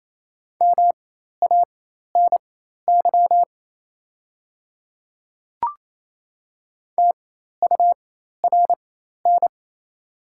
Morse code, heard as a keyed tone, about 28 wpm, spaced as for 13 wpm, 700 Hz. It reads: MANY TURN
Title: 4x Character Spacing - Top 200 Words - 28wpm